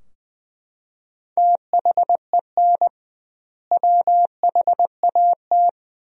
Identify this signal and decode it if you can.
Morse code, heard as a keyed tone, about 20 wpm, 700 Hz. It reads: THEN WHAT
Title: 2x Word-Spacing - Sentences from Top 200 Words - 20wpm